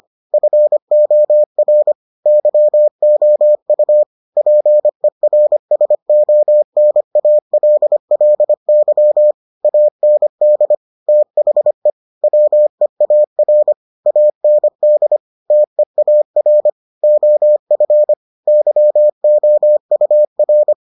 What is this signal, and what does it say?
Morse code, heard as a keyed tone, about 25 wpm, 600 Hz. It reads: FOR YOU PERSONALLY AND THE WEAR AND TEAR OF YOUR